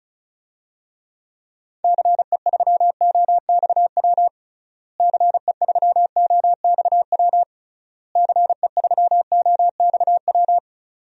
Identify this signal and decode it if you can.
Morse code, heard as a keyed tone, about 34 wpm, 700 Hz. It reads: CE3OXW CE3OXW CE3OXW
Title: DXCC-100 Call Signs - 6 Characters - Repeated 3x - 35wpm